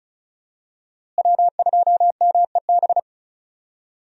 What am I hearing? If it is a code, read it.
Morse code, heard as a keyed tone, about 36 wpm, 700 Hz. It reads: W2MEB
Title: USA Call Signs - 5 Characters - Repeated 2x - 35wpm